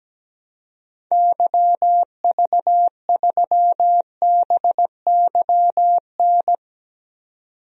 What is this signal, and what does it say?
Morse code, heard as a keyed tone, about 17 wpm, 700 Hz. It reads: YV3BYN
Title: DXCC-100 Call Signs - 6 Characters - 17wpm